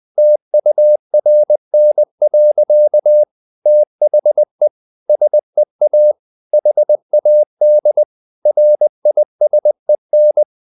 Morse code, heard as a keyed tone, about 20 wpm, 600 Hz. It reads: TURN. THE SEA HAD RISEN
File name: the old man and the sea part 2 20wpm